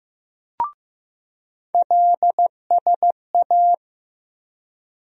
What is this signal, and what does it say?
Morse code, heard as a keyed tone, about 15 wpm, 700 Hz. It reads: LSA
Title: USA Call Sign Postfixes - 3 Letters 15wpm